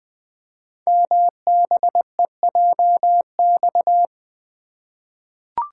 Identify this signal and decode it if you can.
Morse code, heard as a keyed tone, about 20 wpm, 700 Hz. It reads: MBEJX
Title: Code Groups - 5 Letters - 20wpm